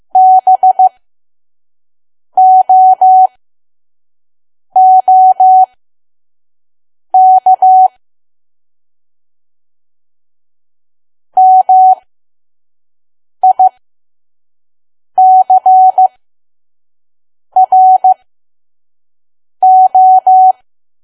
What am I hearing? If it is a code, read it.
Morse code, heard as a keyed tone, about 15 wpm, spaced as for 5 wpm, 750 Hz. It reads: BOOK MICRO
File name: Archive / 5 WPM files / PM